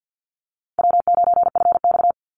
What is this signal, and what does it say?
Morse code, heard as a keyed tone, about 50 wpm, 700 Hz. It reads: W9FX
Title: Easy Call Signs - 50wpm